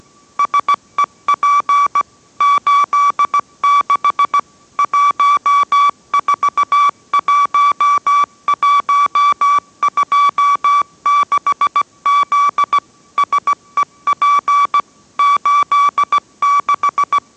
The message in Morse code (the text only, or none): SEP 86 141126Z SEP 86